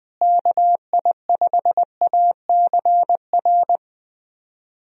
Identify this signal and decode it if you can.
Morse code, heard as a keyed tone, about 21 wpm, 700 Hz. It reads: KI5ACR
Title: USA Call Signs - 6 Characters - Repeated 2x - 20wpm